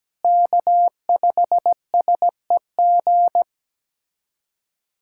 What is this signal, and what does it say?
Morse code, heard as a keyed tone, about 17 wpm, 700 Hz. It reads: K5SEG
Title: USA Callsign DE Callsign - 5 Characters - 17wpm